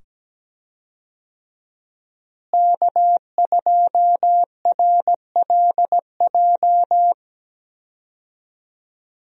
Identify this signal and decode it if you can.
Morse code, heard as a keyed tone, about 17 wpm, 700 Hz. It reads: K2RLJ
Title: USA Call Signs - 5 Characters - Repeated 2x - 17wpm